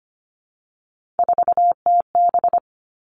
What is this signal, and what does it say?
Morse code, heard as a keyed tone, about 25 wpm, 700 Hz. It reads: 4TB
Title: All Cut Numbers - 3-Digit 25wpm